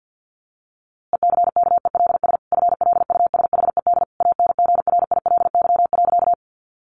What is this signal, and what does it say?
Morse code, heard as a keyed tone, about 50 wpm, 700 Hz. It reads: ECKELS FLUSHED ANGRILY.
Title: A Sound Of Thunder 50wpm